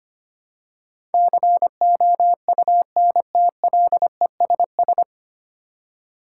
Morse code, heard as a keyed tone, about 25 wpm, 700 Hz. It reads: COUNTLESS
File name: Nine Letter Words - 25wpm